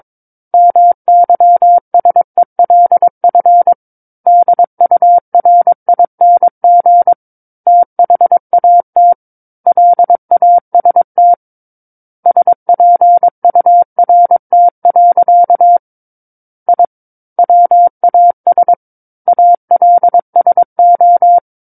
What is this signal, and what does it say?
Morse code, heard as a keyed tone, about 22 wpm, 700 Hz. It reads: MYSELF DURING THAT LAST SPURT. I WAS ALSO
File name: Book One / 13-Chapter 13 - 22 WPM